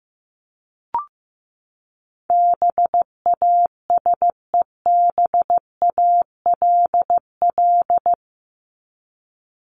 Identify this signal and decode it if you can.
Morse code, heard as a keyed tone, about 15 wpm, 700 Hz. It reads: BASEBALL
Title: Top 1751-2000 Words 15wpm